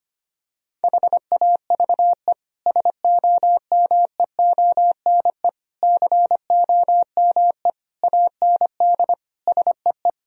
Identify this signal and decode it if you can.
Morse code, heard as a keyed tone, about 25 wpm, 700 Hz. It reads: HAVE SOMEONE COME AND SEE
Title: Sentences from Top 1000 Words - Review Part 4 - 25wpm